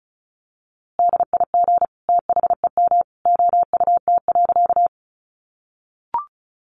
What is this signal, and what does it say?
Morse code, heard as a keyed tone, about 34 wpm, 700 Hz. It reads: DIG THEM OUT.
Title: A Sound Of Thunder 35wpm